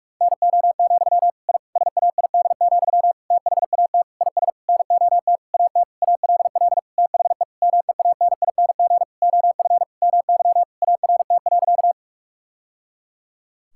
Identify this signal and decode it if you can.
Morse code, heard as a keyed tone, about 46 wpm, 690 Hz. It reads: NO, I SAID, THAT IS NOT AT ALL THE MEANING OF MY ART.
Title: Sentences from Top 1000 Words - Review Part 6 - 45wpm